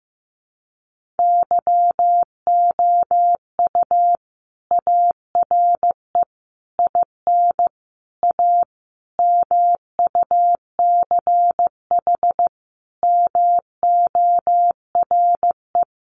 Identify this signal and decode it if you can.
Morse code, heard as a keyed tone, about 15 wpm, 700 Hz. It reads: YOU ARE IN A MUCH MORE
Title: Sentences from Top 1400 Words 15wpm